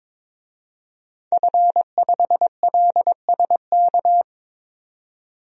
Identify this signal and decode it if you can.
Morse code, heard as a keyed tone, about 22 wpm, 700 Hz. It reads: F5LSK
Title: DXCC-100 Call Signs - 5 Characters - 22wpm